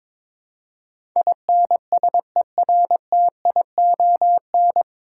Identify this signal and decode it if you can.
Morse code, heard as a keyed tone, about 23 wpm, 700 Hz. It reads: INSERTION
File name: 9 Letter Words - First 3 Letters Given - 22wpm